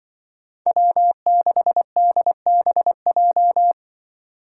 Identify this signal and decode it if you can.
Morse code, heard as a keyed tone, about 25 wpm, 700 Hz. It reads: W6DBJ
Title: Speed Racing - USA 1x3 Call Signs 20wpm